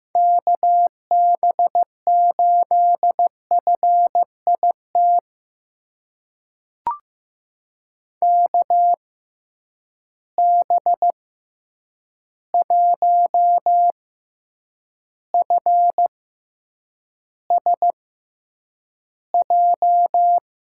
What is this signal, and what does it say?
Morse code, heard as a keyed tone, about 15 wpm, spaced as for 8 wpm, 700 Hz. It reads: KB8FIT KB1FSJ
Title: USA Call Signs - 6 Characters - 6x Character Spacing - 15wpm